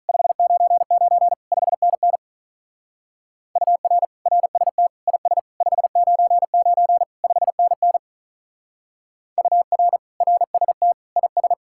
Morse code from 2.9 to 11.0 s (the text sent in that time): UR RST IS 599 5NN UR RST